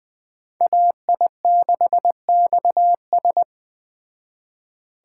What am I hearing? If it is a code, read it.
Morse code, heard as a keyed tone, about 20 wpm, 700 Hz. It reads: AI6XS
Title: Speed Racing - USA 2x2 Call Signs 17wpm